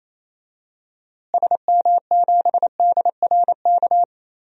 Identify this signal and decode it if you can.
Morse code, heard as a keyed tone, about 27 wpm, 700 Hz. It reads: SM7DRK
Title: DXCC-100 Call Signs - 6 Characters - Repeated 2x - 28wpm